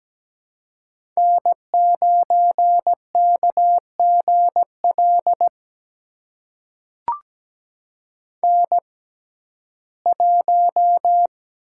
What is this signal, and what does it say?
Morse code, heard as a keyed tone, about 17 wpm, spaced as for 12 wpm, 700 Hz. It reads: N9KGL N1
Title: USA Call Signs - 6x Character Spacing - 17wpm